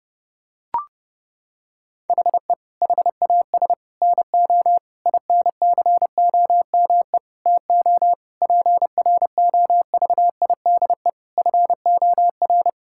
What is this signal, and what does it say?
Morse code, heard as a keyed tone, about 30 wpm, 700 Hz. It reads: HE HAS NO INCOME TO PROVIDE FOR
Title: Sentences from Top 1000 Words - Review Part 5 - 30wpm